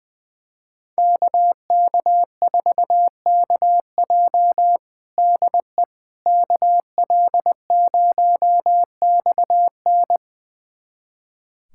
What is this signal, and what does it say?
Morse code, heard as a keyed tone, about 20 wpm, 700 Hz. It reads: KK4KJ DE KL0XN
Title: USA Callsign DE Callsign - 5 Characters - 20wpm